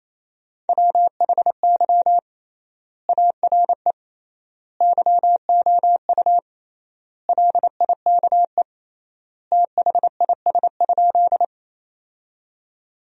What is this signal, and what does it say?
Morse code, heard as a keyed tone, about 28 wpm, 700 Hz. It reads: WHY ARE YOU LIKE THIS?
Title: 3x Word-Spacing - Sentences from Top 200 Words - 28wpm